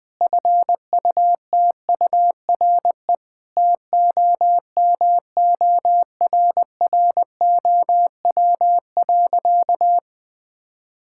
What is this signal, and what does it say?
Morse code, heard as a keyed tone, about 20 wpm, 700 Hz. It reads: FUTURE TOMORROW.
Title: Difficult Sentences with Hints - 20wpm